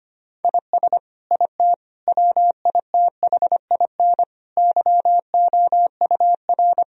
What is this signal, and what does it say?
Morse code, heard as a keyed tone, about 25 wpm, 700 Hz. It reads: IS IT WITHIN YOUR
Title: Sentences from Top 1000 Words - Review Part 4 - 25wpm